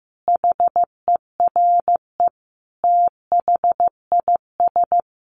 Morse code, heard as a keyed tone, about 15 wpm, 700 Hz. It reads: HERE THIS
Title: Sentences from Top 1500 Words 15wpm